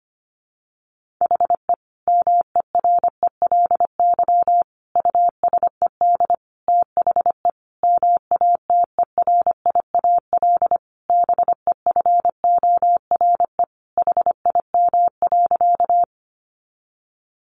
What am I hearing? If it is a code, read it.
Morse code, heard as a keyed tone, about 25 wpm, 700 Hz. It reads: HE MERELY USED THE MATERIAL BEFORE HIM.
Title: Sentences from Top 1400 Words 25wpm